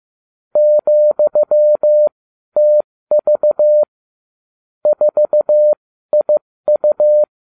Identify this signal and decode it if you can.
Morse code, heard as a keyed tone, about 15 wpm, spaced as for 12 wpm, 600 Hz. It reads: ,TV 4IU